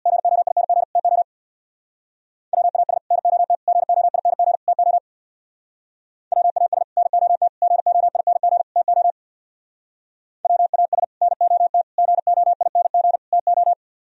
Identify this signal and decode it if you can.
Morse code, heard as a keyed tone, about 64 wpm, 700 Hz. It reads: GOING TO WAS NOT GOING TO WAS NOT GOING TO WAS NOT GOING TO